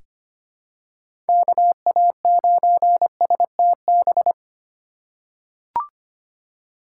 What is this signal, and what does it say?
Morse code, heard as a keyed tone, about 25 wpm, 700 Hz. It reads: KA9STB
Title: USA Call Signs - 3x Character Spacing - 25wpm